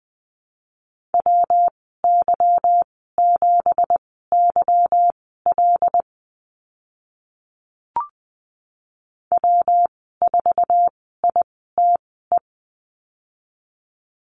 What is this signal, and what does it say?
Morse code, heard as a keyed tone, about 20 wpm, spaced as for 15 wpm, 700 Hz. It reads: WY7YL W4ITE
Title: 173 Lesson 28g - Number 9 - Call Signs 20wpm